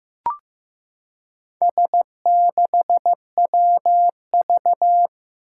Morse code, heard as a keyed tone, about 15 wpm, 700 Hz. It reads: S6WV
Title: Four Challenging Letter-Number Combinations 15wpm